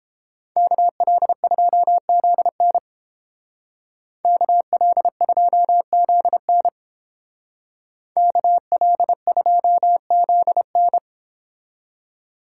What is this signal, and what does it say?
Morse code, heard as a keyed tone, about 31 wpm, 700 Hz. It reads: KL2ZN KL2ZN KL2ZN